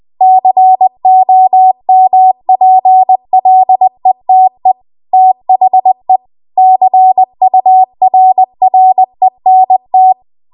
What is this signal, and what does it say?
Morse code, heard as a keyed tone, about 20 wpm, 750 Hz. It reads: COMPLETE THE CURRENT
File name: Archive / 20 WPM / 20WPM